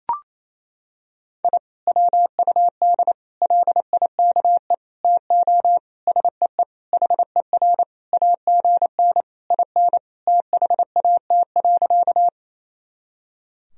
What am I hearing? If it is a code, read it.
Morse code, heard as a keyed tone, about 28 wpm, 700 Hz. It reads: I WUD LIKE TO SEE HER AGN IN THAT.
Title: Sentences from Top 200 Words and Abbreviations 28wpm